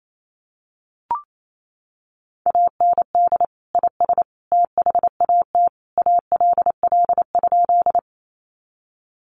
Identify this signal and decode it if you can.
Morse code, heard as a keyed tone, about 28 wpm, 700 Hz. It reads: AND IS THAT ALL?
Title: Sentences from Top 100 Words 28wpm